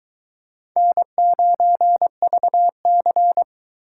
Morse code, heard as a keyed tone, about 23 wpm, 700 Hz. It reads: N9VC